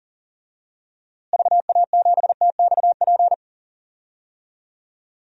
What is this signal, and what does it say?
Morse code, heard as a keyed tone, about 40 wpm, 700 Hz. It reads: VA7TXP
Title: DXCC-100 Call Signs - 6 Characters - Repeated 2x - 40wpm